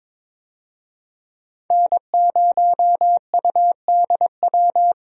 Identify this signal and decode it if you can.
Morse code, heard as a keyed tone, about 22 wpm, 700 Hz. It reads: N0UDW
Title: USA Call Signs - 5 Characters - 22wpm